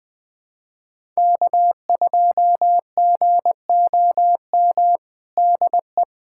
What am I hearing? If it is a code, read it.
Morse code, heard as a keyed tone, about 20 wpm, 700 Hz. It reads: K2GOM DE
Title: USA Callsign DE Callsign - 5 Characters - 20wpm